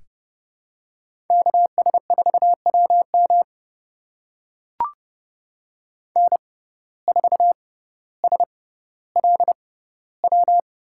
Morse code, heard as a keyed tone, about 30 wpm, spaced as for 15 wpm, 700 Hz. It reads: KS4WM N4SLW
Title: USA Call Signs - 6x Character Spacing - 30wpm